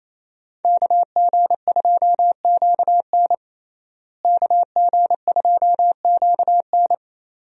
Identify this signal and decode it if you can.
Morse code, heard as a keyed tone, about 28 wpm, 700 Hz. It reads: KG2QN KG2QN
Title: USA Call Signs - 5 Characters - Repeated 2x - 28wpm